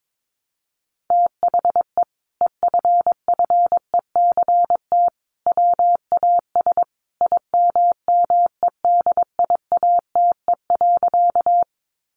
Morse code, heard as a keyed tone, about 22 wpm, 700 Hz. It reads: THE EFFECT WAS IMMEDIATE.
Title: Sentences from Top 1200 Words 22wpm